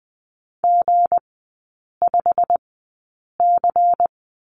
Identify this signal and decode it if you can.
Morse code, heard as a keyed tone, about 20 wpm, spaced as for 8 wpm, 700 Hz. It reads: G5C
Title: 198 Lesson 32e - Letter k - Three Characters 20wpm